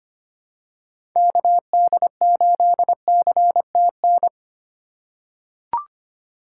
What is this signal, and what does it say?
Morse code, heard as a keyed tone, about 25 wpm, 700 Hz. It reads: KD8CTN
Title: USA Call Signs - 6 Characters - 3x Character Spacing - 25wpm